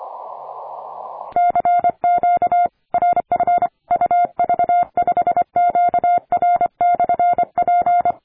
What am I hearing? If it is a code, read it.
Morse code, heard as a keyed tone, about 25 wpm, 700 Hz. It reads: CQ RF UV5QR/P
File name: attachments / 40m UU7JF P